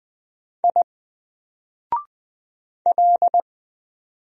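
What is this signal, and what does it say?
Morse code, heard as a keyed tone, about 20 wpm, 700 Hz. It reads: I L
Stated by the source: Single letter and number Morse drill at 20 wpm, from N4LCD.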